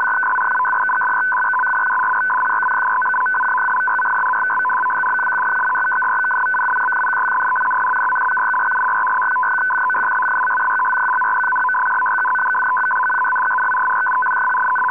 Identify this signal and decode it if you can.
Two-tone teleprinter signal, about 500 Hz shift.